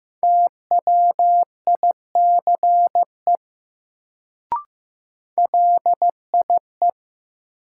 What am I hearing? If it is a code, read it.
Morse code, heard as a keyed tone, about 15 wpm, 700 Hz. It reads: TWICE LIE
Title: Sentences from Top 1600 Words - New Word Drill - 15wpm